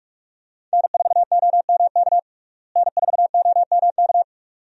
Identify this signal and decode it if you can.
Morse code, heard as a keyed tone, about 46 wpm, 700 Hz. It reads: N4OMK N4OMK
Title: USA Call Signs - 5 Characters - Repeated 3x - 45wpm